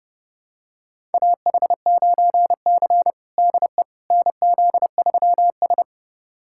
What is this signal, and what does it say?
Morse code, heard as a keyed tone, about 30 wpm, 700 Hz. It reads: AH9C DE NZ3S
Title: USA Callsign DE Callsign - 4 Characters - 30wpm